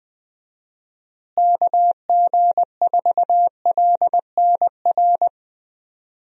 Morse code, heard as a keyed tone, about 20 wpm, 700 Hz. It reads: KG4LNR
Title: USA Call Signs - 6 Characters - Repeated 3x - 20wpm